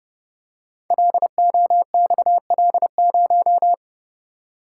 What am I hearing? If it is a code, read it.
Morse code, heard as a keyed tone, about 28 wpm, 700 Hz. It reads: LOXL0
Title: Code Groups - 5 Letter:Numbers - 30wpm